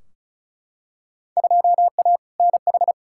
Morse code, heard as a keyed tone, about 36 wpm, 700 Hz. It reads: 2A NH
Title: Field Day Exchanges - 35wpm